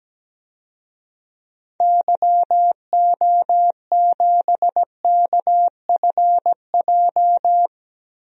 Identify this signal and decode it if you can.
Morse code, heard as a keyed tone, about 17 wpm, 700 Hz. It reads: YO7KFJ
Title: DXCC-100 Call Signs - 6 Characters - Repeated 2x - 17wpm